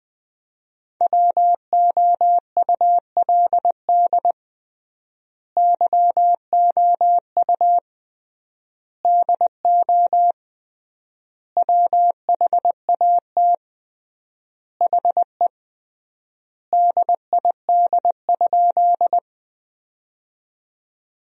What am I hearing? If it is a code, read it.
Morse code, heard as a keyed tone, about 21 wpm, 700 Hz. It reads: WOULD YOU DO WHAT HE DID?
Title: 3x Word-Spacing - Sentences from Top 100 Words - 20wpm